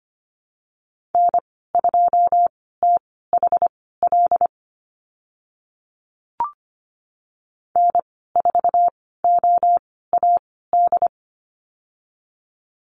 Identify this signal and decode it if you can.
Morse code, heard as a keyed tone, about 25 wpm, spaced as for 16 wpm, 690 Hz. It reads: N2THL N4OAD
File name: Lesson 15g - Number 5 - Call Signs 25wpm